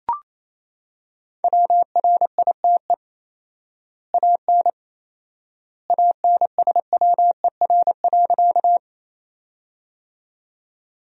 Morse code, heard as a keyed tone, about 28 wpm, 700 Hz. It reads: WRITE AN ANSWER.
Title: 4x Word-Spacing - Sentences from Top 300 Words - 28wpm